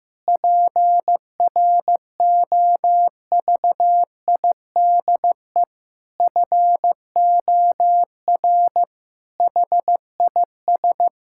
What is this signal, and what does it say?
Morse code, heard as a keyed tone, about 15 wpm, 700 Hz. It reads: PROVIDE FOR HIS
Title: Sentences from Top 1600 Words 15wpm